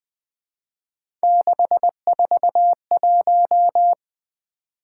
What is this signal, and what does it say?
Morse code, heard as a keyed tone, about 23 wpm, 700 Hz. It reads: 641
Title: Speed-Racing - Three Letter-Number Combinations - 20wpm